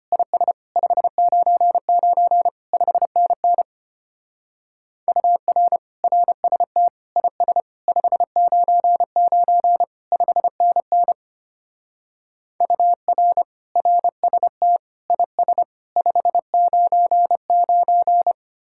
Morse code, heard as a keyed tone, about 31 wpm, 700 Hz. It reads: IS 599 5NN UR RST IS 599 5NN UR RST IS 599
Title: Speed-Racing Common QSO Phrases - 25wpm